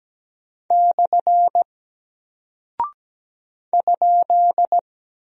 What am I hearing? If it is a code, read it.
Morse code, heard as a keyed tone, about 17 wpm, 700 Hz. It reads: / ?
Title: Single Character (Letter:Number:Punctuation:Prosign) Alt 17wpm